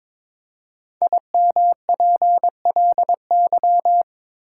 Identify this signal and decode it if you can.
Morse code, heard as a keyed tone, about 22 wpm, 700 Hz. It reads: IMPLY
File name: 5 Letter Words - First 3 Letters Given - 22wpm